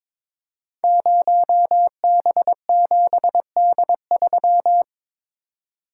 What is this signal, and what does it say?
Morse code, heard as a keyed tone, about 22 wpm, 700 Hz. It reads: 0B7D3
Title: Code Groups - 5 Letter:Numbers - 22wpm